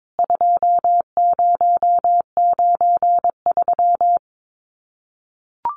Morse code, heard as a keyed tone, about 22 wpm, 700 Hz. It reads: 2093
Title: CWT Exchange 22wpm